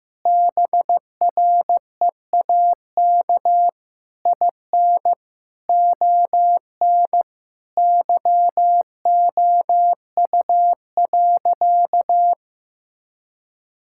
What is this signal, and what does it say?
Morse code, heard as a keyed tone, about 16 wpm, 700 Hz. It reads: BREAK IN ON YOU.
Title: Sentences from Top 1000 Words - Review Part 6 - 15wpm